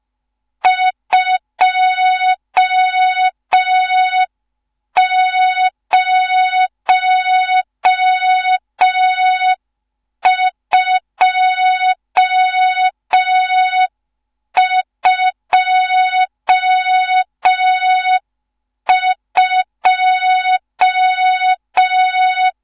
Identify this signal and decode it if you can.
Morse code, heard as a keyed tone, about 5 wpm, 3700 Hz, with noise. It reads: 20222
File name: Morze-HA2MN / hang / 14 280 HA2MN